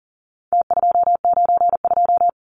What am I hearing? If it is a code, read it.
Morse code, heard as a keyed tone, about 40 wpm, 700 Hz. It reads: T292
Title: Numbers - Contest Style - 4-Digit 40wpm